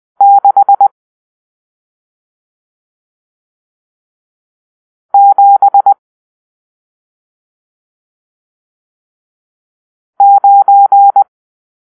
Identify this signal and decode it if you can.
Morse code, heard as a keyed tone, about 20 wpm, 800 Hz. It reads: 6 7 9
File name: arrl / Grp-7 Drill-1